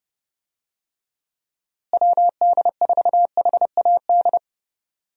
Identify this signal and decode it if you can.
Morse code, heard as a keyed tone, about 32 wpm, 700 Hz. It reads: WD4HAD
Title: USA Call Signs - 6 Characters - Repeated 3x - 30wpm